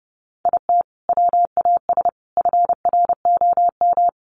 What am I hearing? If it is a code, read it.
Morse code, heard as a keyed tone, about 30 wpm, 700 Hz. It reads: IT WAS FROM
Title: Sentences from Top 100 Words 30wpm